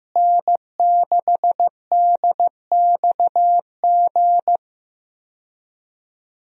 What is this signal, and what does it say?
Morse code, heard as a keyed tone, about 15 wpm, 700 Hz. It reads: N6DXG
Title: USA Call Signs - 5 Characters - Repeated 3x - 15wpm